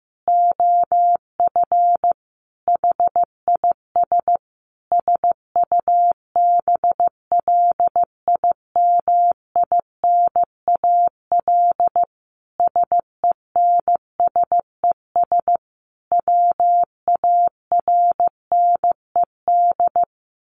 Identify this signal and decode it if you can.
Morse code, heard as a keyed tone, about 15 wpm, 700 Hz. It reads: OF HIS SUBLIMINAL SENSES WARNED
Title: A Sound Of Thunder 15wpm